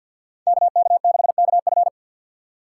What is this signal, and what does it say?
Morse code, heard as a keyed tone, about 49 wpm, 700 Hz. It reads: KK6KF
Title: USA Call Signs - 5 Characters - 3x Character Spacing - 50wpm